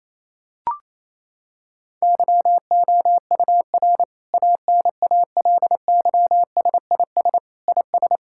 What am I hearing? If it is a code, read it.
Morse code, heard as a keyed tone, about 28 wpm, 700 Hz. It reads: YOUR ANALYSIS IS
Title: Sentences from Top 1000 Words - Review Part 5 - 28wpm